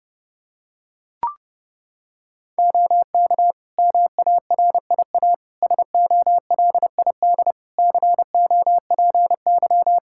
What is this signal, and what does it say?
Morse code, heard as a keyed tone, about 32 wpm, 700 Hz. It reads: OK MARIA SOLID COPY
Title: Practice QSO - Exchange 3 - More Info - 30wpm